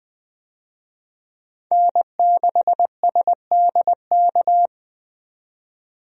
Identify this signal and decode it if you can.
Morse code, heard as a keyed tone, about 20 wpm, 700 Hz. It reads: N6SDK
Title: USA Call Signs - 5 Characters - 20wpm